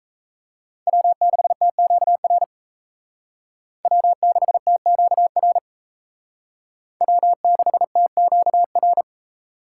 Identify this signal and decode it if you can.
Morse code, heard as a keyed tone, about 35 wpm, 700 Hz. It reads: W6TQR W6TQR W6TQR